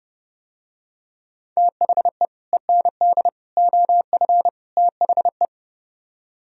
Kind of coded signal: Morse code

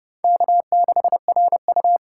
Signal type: Morse code

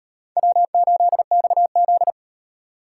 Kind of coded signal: Morse code

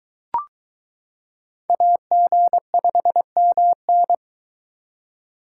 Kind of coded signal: Morse code

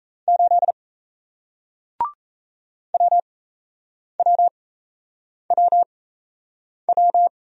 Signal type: Morse code

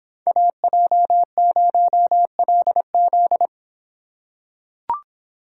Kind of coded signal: Morse code

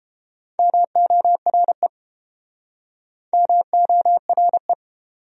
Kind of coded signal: Morse code

